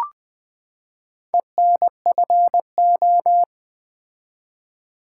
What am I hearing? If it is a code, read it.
Morse code, heard as a keyed tone, about 20 wpm, 700 Hz. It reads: ENFO
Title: Build A Word - 4 Letters At A Time - 11-Letter Words - 20wpm